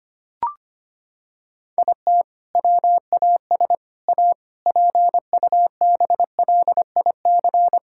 Morse code, heard as a keyed tone, about 25 wpm, 700 Hz. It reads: IT WAS A PUBLIC
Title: Sentences from Top 1600 Words 25wpm